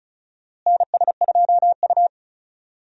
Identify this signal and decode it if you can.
Morse code, heard as a keyed tone, about 34 wpm, 700 Hz. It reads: NS2U
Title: USA Call Signs - 4 Characters - Repeated 3x - 35wpm